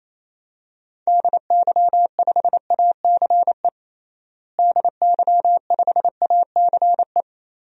Morse code, heard as a keyed tone, about 28 wpm, 700 Hz. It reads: DY5ACE DY5ACE
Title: DXCC-100 Call Signs - 6 Characters - Repeated 2x - 28wpm